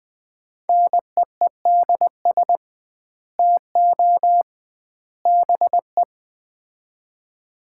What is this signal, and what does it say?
Morse code, heard as a keyed tone, about 20 wpm, 700 Hz. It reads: NEEDS TO BE
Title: 2x Word-Spacing - Sets of 3 Words - 20wpm